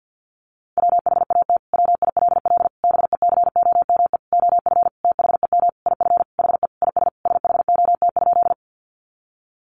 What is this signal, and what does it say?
Morse code, heard as a keyed tone, about 50 wpm, 700 Hz. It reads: WHAT WILL BECOME OF THEM IF HE IS SHOT?
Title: Sentences from Top 1100 Words 50wpm